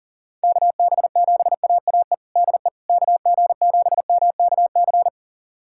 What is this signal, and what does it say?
Morse code, heard as a keyed tone, about 40 wpm, 700 Hz. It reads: KB7AAE DE KG7MKC
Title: USA Callsign DE Callsign - 6 Characters - 40wpm